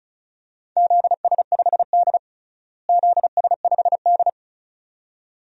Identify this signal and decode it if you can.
Morse code, heard as a keyed tone, about 34 wpm, 700 Hz. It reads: ZS5D ZS5D
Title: DXCC-100 Call Signs - 4 Characters - Repeated 2x - 35wpm